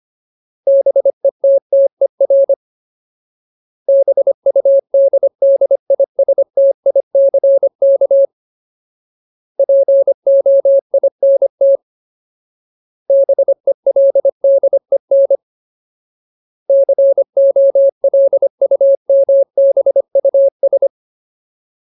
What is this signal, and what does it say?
Morse code, heard as a keyed tone, about 25 wpm, 550 Hz. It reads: BETTER BUDDISTICK POINT BELDEN COLUMBUS
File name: Ultimate Word List / 25WPM Ultimate